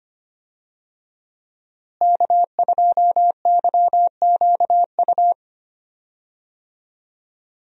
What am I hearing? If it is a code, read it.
Morse code, heard as a keyed tone, about 25 wpm, 700 Hz. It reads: K2YQU
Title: USA Call Signs - 5 Characters - Repeated 3x - 25wpm